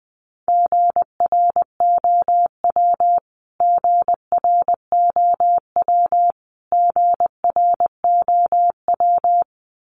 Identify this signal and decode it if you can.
Morse code, heard as a keyed tone, about 19 wpm, 700 Hz. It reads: GROW GROW GROW
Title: Top 300 Words Repeated 20wpm